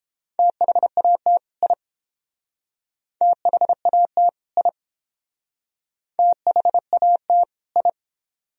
Morse code, heard as a keyed tone, about 31 wpm, 700 Hz. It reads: THAT I THAT I THAT I